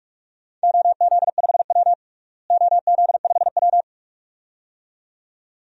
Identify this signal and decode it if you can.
Morse code, heard as a keyed tone, about 46 wpm, 700 Hz. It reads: OZ5W OZ5W
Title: DXCC-100 Call Signs - 4 Characters - Repeated 2x - 45wpm